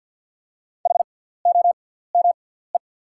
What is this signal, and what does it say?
Morse code, heard as a keyed tone, about 49 wpm, spaced as for 17 wpm, 690 Hz. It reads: HOME
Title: 6x Character Spacing - Top 500 Words - 50wpm